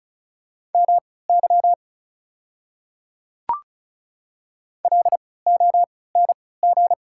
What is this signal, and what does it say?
Morse code, heard as a keyed tone, about 36 wpm, spaced as for 20 wpm, 700 Hz. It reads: MY LONG
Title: 3x Character Spacing - Top 200 Words - 35wpm